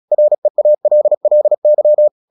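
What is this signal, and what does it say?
Morse code, heard as a keyed tone, about 36 wpm, 600 Hz. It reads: REALLY
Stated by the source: Morse practice at 36 wpm, text of Ken Follett's novel Never, from CW med Gnister.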